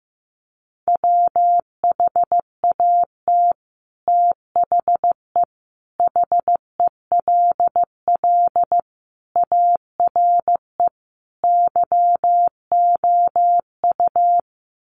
Morse code, heard as a keyed tone, about 15 wpm, 700 Hz. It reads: WHAT THE HELL ARE YOU
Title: Sets of 5 Words Encore 15wpm